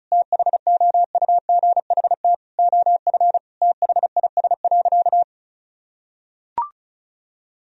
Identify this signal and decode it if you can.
Morse code, heard as a keyed tone, about 34 wpm, 700 Hz. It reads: THOUGHT OF THIS.
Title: Sentences from Top 1000 Words - Review Part 1 - 35wpm